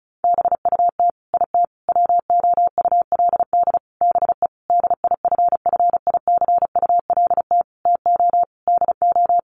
Morse code, heard as a keyed tone, about 34 wpm, 700 Hz. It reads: BUT IT WOULD BE DIFFICULT TO DO
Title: Sentences from Top 600 Words 35wpm